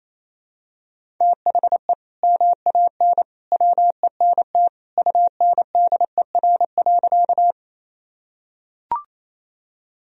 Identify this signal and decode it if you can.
Morse code, heard as a keyed tone, about 28 wpm, 700 Hz. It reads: THE MAN WENT UNDER.
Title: Sentences from Top 1000 Words - Review Part 1 - 28wpm